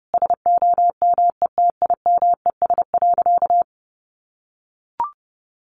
Morse code, heard as a keyed tone, about 30 wpm, 700 Hz. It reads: SOMETIMES.
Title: Sentences from Top 1400 Words 30wpm